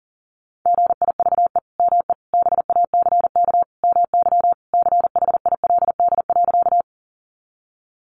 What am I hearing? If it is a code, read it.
Morse code, heard as a keyed tone, about 40 wpm, 700 Hz. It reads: GIVE ME BACK MY CHILD.
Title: Sentences from Top 800 Words 40wpm